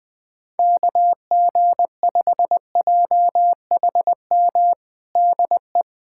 Morse code, heard as a keyed tone, about 20 wpm, 700 Hz. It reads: KG5JHM DE
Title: USA Callsign DE Callsign - 6 Characters - 20wpm